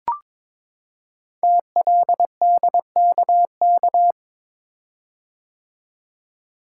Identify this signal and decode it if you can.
Morse code, heard as a keyed tone, about 22 wpm, 700 Hz. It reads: TLDKK
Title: Code Groups - 5 Letters - 22wpm